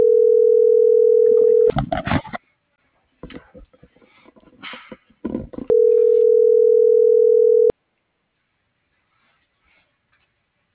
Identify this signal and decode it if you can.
Ringback tone (North America), 440 + 480 Hz.